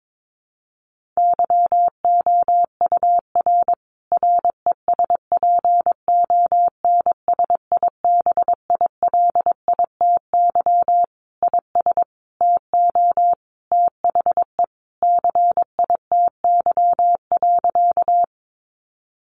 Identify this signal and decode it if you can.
Morse code, heard as a keyed tone, about 22 wpm, 700 Hz. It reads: YOUR RESPONSIBILITY IS TO THE CITY.
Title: Sentences from Top 1200 Words 22wpm